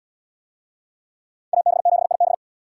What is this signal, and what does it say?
Morse code, heard as a keyed tone, about 77 wpm, 700 Hz. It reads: AB7IB